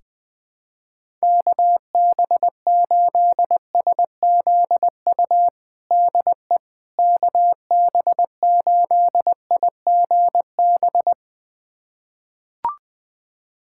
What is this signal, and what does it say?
Morse code, heard as a keyed tone, about 20 wpm, 700 Hz. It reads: KB8SZU DE KB8IGB
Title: USA Callsign DE Callsign - 6 Characters - 20wpm